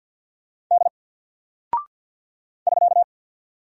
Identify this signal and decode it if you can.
Morse code, heard as a keyed tone, about 48 wpm, 700 Hz. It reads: D <SK>
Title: Single Character (Letter:Number:Punctuation:Prosign) 50wpm